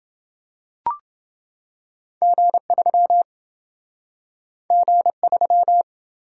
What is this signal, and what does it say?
Morse code, heard as a keyed tone, about 27 wpm, 700 Hz. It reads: G3 G3